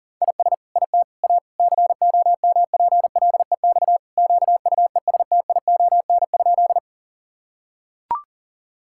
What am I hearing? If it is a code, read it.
Morse code, heard as a keyed tone, about 37 wpm, 700 Hz. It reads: IS IT A COMPLEX QUESTION?
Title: Sentences from Top 1000 Words - Review Part 6 - 40wpm